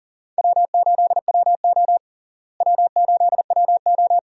Morse code, heard as a keyed tone, about 37 wpm, 700 Hz. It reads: W8WO W8WO
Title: USA Call Signs - 4 Characters - Repeated 2x - 40wpm